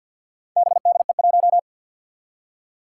Morse code, heard as a keyed tone, about 47 wpm, 700 Hz. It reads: BDE1
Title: Four Challenging Letter-Number Combinations 50wpm